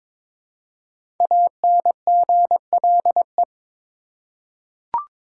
Morse code, heard as a keyed tone, about 22 wpm, 700 Hz. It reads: ANGLE
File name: 5 Letter Words - First 3 Letters Given - 22wpm